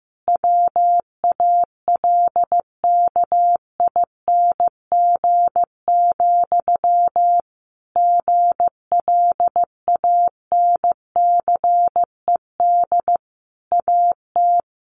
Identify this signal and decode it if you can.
Morse code, heard as a keyed tone, about 14 wpm, 700 Hz. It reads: WALKING, GLANCED AT
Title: A Sound Of Thunder 15wpm